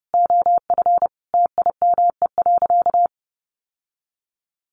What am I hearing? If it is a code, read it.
Morse code, heard as a keyed tone, about 30 wpm, 700 Hz. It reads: OF TIME.
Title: Sentences from Top 1300 Words 30wpm